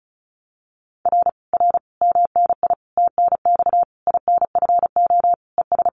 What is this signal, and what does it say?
Morse code, heard as a keyed tone, about 34 wpm, 700 Hz. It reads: R R MNI TNX INFO ES
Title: Common QSO Phrases Encore 35wpm